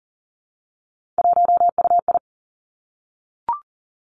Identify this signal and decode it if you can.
Morse code, heard as a keyed tone, about 41 wpm, 700 Hz. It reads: 1UI
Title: Three Challenging Letter-Number Combinations 40wpm